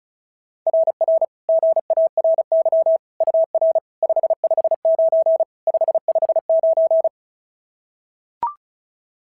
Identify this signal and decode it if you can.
Morse code, heard as a keyed tone, about 36 wpm, 690 Hz. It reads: RR GARY UR 559 559
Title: SOTA QSO Practice 35wpm